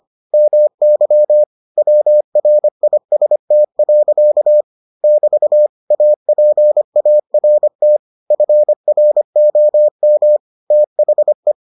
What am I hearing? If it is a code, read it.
Morse code, heard as a keyed tone, about 25 wpm, 600 Hz. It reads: MY WRIST. <BT> APART FROM THE